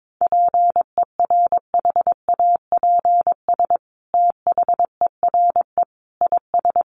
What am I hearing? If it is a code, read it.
Morse code, heard as a keyed tone, about 22 wpm, 700 Hz. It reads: PERHAPS THERE IS
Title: Sentences from Top 600 Words 22wpm